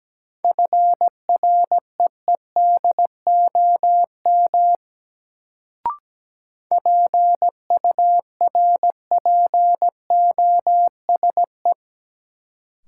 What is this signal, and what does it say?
Morse code, heard as a keyed tone, about 17 wpm, 700 Hz. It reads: FREEDOM PURPOSE
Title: Sentences from Top 800 Words - New Word Drill - 17wpm